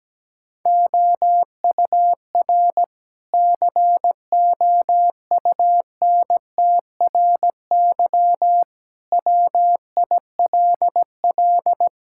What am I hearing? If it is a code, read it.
Morse code, heard as a keyed tone, about 17 wpm, 700 Hz. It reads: OUR COUNTRY WILL
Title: Sentences from Top 1000 Words - Review Part 6 - 17wpm